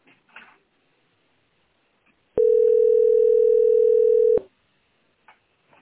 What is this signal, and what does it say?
Ringback tone (North America), 440 + 480 Hz.